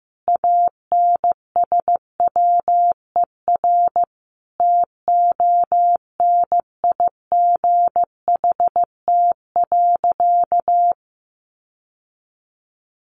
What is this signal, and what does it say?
Morse code, heard as a keyed tone, about 15 wpm, 700 Hz. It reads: ANSWER TONIGHT.
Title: Sentences from Top 600 Words 15wpm